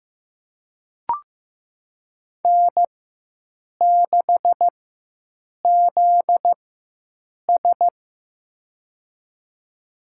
Morse code, heard as a keyed tone, about 15 wpm, spaced as for 7 wpm, 700 Hz. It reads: N6ZS
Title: USA Call Signs - 4 Characters - 4x Character Spacing - 15wpm